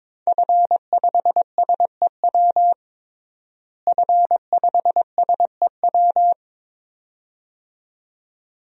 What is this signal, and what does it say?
Morse code, heard as a keyed tone, about 22 wpm, 700 Hz. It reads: F5SEW F5SEW
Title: DXCC-100 Call Signs - 5 Characters - Repeated 2x - 22wpm